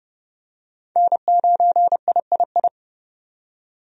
Morse code, heard as a keyed tone, about 32 wpm, 700 Hz. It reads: N9III
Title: USA Call Signs - 5 Characters - 30wpm